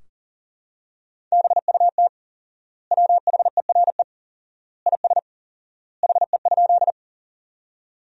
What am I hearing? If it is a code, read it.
Morse code, heard as a keyed tone, about 40 wpm, 700 Hz. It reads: BUT WHERE IS HE?
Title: 4x Word-Spacing - Sentences from Top 200 Words - 40wpm